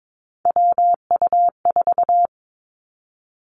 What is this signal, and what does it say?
Morse code, heard as a keyed tone, about 22 wpm, 700 Hz. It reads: WU4
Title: Three Challenging Letter-Number Combinations 22wpm